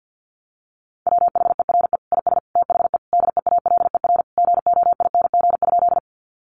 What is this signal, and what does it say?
Morse code, heard as a keyed tone, about 50 wpm, 700 Hz. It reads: WHERE IS THE DEALER GOING?
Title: Sentences from Top 1100 Words 50wpm